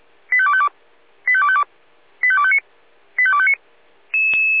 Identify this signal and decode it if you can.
Five-tone selective call (ZVEI), heard as quick runs of five tones, tones of about 70 ms. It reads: 85252 85252 85259 85259
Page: /496/SDR/ZVEI/85250/